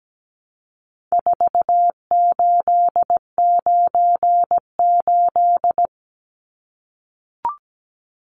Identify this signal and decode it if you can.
Morse code, heard as a keyed tone, about 17 wpm, 700 Hz. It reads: 4898
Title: Numbers - 4-Digit 17wpm